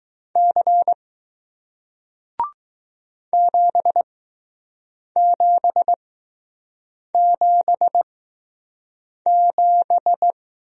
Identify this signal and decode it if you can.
Morse code, heard as a keyed tone, about 20 wpm, spaced as for 6 wpm, 700 Hz. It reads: C 7777